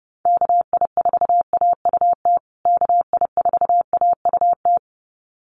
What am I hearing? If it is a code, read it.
Morse code, heard as a keyed tone, about 30 wpm, 700 Hz. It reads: KI4AUT KI4AUT
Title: USA Call Signs Repeated 2x 30wpm